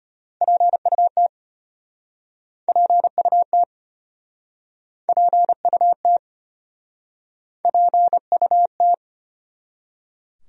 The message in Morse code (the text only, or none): PUT PUT PUT PUT